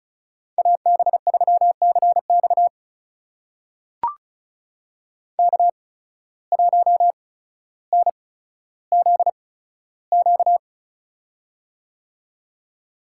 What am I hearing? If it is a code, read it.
Morse code, heard as a keyed tone, about 36 wpm, spaced as for 15 wpm, 700 Hz. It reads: AB3CX K1NZQ
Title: USA Call Signs - 5 Characters - 8x Character Spacing - 35wpm